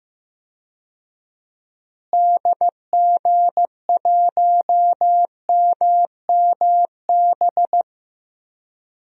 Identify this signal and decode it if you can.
Morse code, heard as a keyed tone, about 15 wpm, 700 Hz. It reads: DG1MMB
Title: DXCC-100 Call Signs - 6 Characters - Repeated 2x - 15wpm